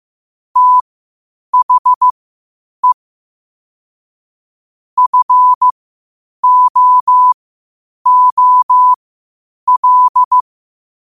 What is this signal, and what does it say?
Morse code, heard as a keyed tone, about 15 wpm, spaced as for 8 wpm, 1000 Hz. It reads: THE FOOL
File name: qotdfiles / 8wpm